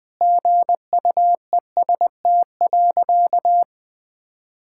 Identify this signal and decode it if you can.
Morse code, heard as a keyed tone, about 20 wpm, 700 Hz. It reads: GUEST.
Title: Sentences from Top 1600 Words 20wpm